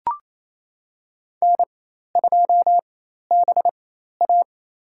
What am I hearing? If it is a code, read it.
Morse code, heard as a keyed tone, about 27 wpm, spaced as for 13 wpm, 700 Hz. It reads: N2BA
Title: USA Call Signs - 4 Characters - 4x Character Spacing - 28wpm